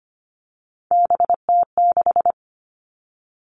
Morse code, heard as a keyed tone, about 25 wpm, 700 Hz. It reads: BT6
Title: All Cut Numbers - 3-Digit 25wpm